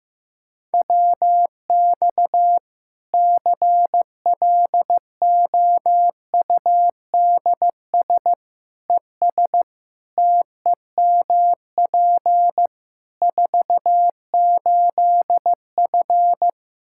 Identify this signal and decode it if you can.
Morse code, heard as a keyed tone, about 15 wpm, 700 Hz. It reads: WX CLOUDS ES TEMP 48F
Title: Practice QSO - Exchange 2 - Info - 15wpm